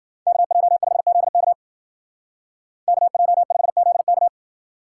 Morse code, heard as a keyed tone, about 50 wpm, 700 Hz. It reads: KJ5ZK KJ5ZK
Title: Speed Racing - USA 2x2 Call Signs 40wpm